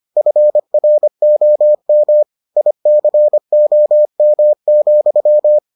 Morse code, heard as a keyed tone, about 25 wpm, 600 Hz. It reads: FROM ICOM,